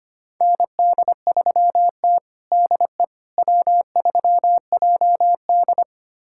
Morse code, heard as a keyed tone, about 25 wpm, 700 Hz. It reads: ND3T DE W3JD
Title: USA Callsign DE Callsign - 4 Characters - 25wpm